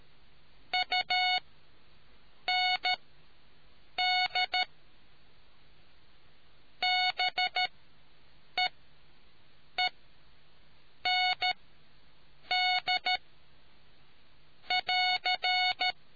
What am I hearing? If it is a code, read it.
Morse code, heard as a keyed tone, about 13 wpm, spaced as for 6 wpm, 730 Hz. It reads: UND BEEND <AR>